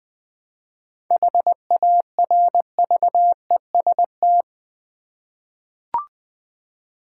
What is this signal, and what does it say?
Morse code, heard as a keyed tone, about 20 wpm, 700 Hz. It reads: HARVEST
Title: 7 Letter Words - First 3 Letters Given - 20wpm